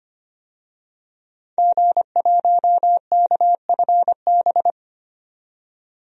Morse code, heard as a keyed tone, about 25 wpm, 700 Hz. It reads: G1KFB
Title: DXCC-100 Call Signs - 5 Characters - 25wpm